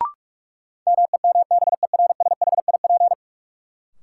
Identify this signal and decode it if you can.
Morse code, heard as a keyed tone, about 41 wpm, 700 Hz. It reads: MEMBERSHIP
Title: Sentences from Top 1400 Words - New Word Drill Alt - 45wpm